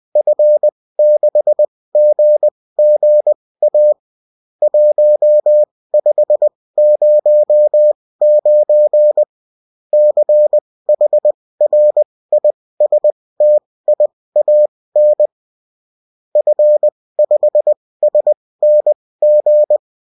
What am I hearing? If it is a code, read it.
Morse code, heard as a keyed tone, about 20 wpm, spaced as for 16 wpm, 600 Hz. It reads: F6GGA 1509 CHRISTIAN F5SNG
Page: bulletin-uft-sous-forme-texte-et-en-cw